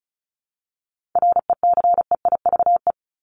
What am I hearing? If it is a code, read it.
Morse code, heard as a keyed tone, about 34 wpm, 700 Hz. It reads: RECEIVE
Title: ING Suffix - Top 100 Words with Base Words 35wpm